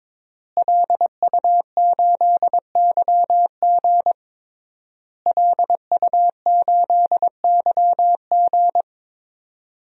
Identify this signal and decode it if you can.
Morse code, heard as a keyed tone, about 22 wpm, 700 Hz. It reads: LU8YG LU8YG
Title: DXCC-100 Call Signs - 5 Characters - Repeated 3x - 22wpm